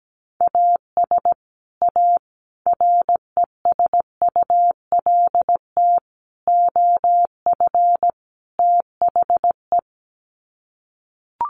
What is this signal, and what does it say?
Morse code, heard as a keyed tone, about 17 wpm, 700 Hz. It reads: AS A RESULT OF THE
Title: Sets of 5 Words 17wpm